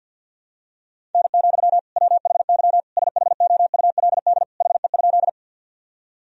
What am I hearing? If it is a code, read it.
Morse code, heard as a keyed tone, about 52 wpm, 700 Hz. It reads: N, WHY SHOULD HE?
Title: Sentences from Top 300 Words and Abbreviations 50wpm